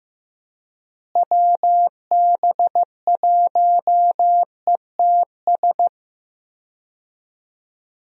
Morse code, heard as a keyed tone, about 15 wpm, 700 Hz. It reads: WB1ETS
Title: USA Call Signs - 6 Characters - Repeated 2x - 15wpm